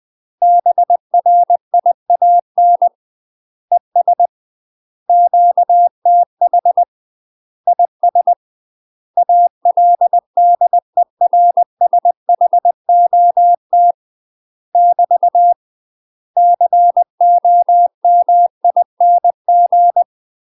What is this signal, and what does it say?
Morse code, heard as a keyed tone, about 20 wpm, 700 Hz. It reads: BRIAN ES QTH IS ALDERSHOT <BT> COMING
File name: UK MASTER QSOs TRAINING / 20WPM / UK MASTER QSOs 0060